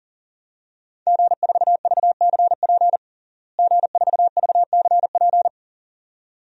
Morse code, heard as a keyed tone, about 40 wpm, 700 Hz. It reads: G4VCP G4VCP
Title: DXCC-100 Call Signs - 5 Characters - Repeated 2x - 40wpm